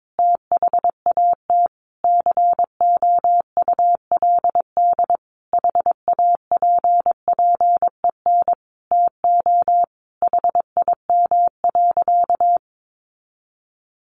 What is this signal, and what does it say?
Morse code, heard as a keyed tone, about 22 wpm, 700 Hz. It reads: THAT COULD HAPPEN TO HIM.
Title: Sentences from Top 400 Words 22wpm